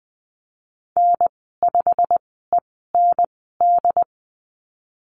Morse code, heard as a keyed tone, about 20 wpm, spaced as for 15 wpm, 700 Hz. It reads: N5END
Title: 173 Lesson 28g - Number 9 - Call Signs 20wpm